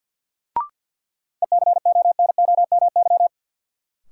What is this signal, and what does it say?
Morse code, heard as a keyed tone, about 53 wpm, 700 Hz. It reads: ECONOMY
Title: Sentences from Top 1000 Words - Review Part 3 - New Word Drill - 50wpm